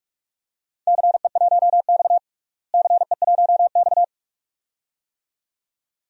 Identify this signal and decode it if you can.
Morse code, heard as a keyed tone, about 41 wpm, 700 Hz. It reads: CE1X CE1X
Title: DXCC-100 Call Signs - 4 Characters - Repeated 2x - 45wpm